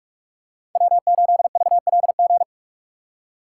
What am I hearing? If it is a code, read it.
Morse code, heard as a keyed tone, about 46 wpm, 700 Hz. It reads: W8VLG
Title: USA Call Signs - 3x Character Spacing - 45wpm